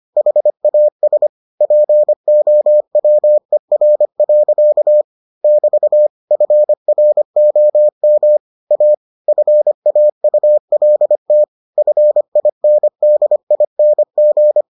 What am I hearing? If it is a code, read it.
Morse code, heard as a keyed tone, about 25 wpm, 600 Hz. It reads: HAS POWER. <BT> FROM A FAULT FINDING